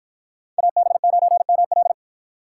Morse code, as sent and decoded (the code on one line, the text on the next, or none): .- -... ----. -- .-..
AB9ML